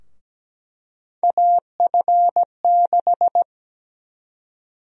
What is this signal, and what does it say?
Morse code, heard as a keyed tone, about 17 wpm, 700 Hz. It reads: AF6
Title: USA Call Sign Prefixes - 17wpm